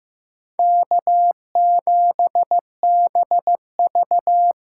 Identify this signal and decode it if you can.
Morse code, heard as a keyed tone, about 15 wpm, 700 Hz. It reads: K7BV
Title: Practice QSO - Exchange 4 - Ending - 15wpm